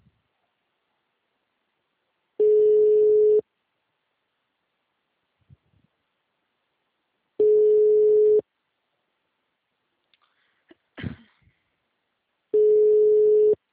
Ringback tone (Europe, 425 Hz).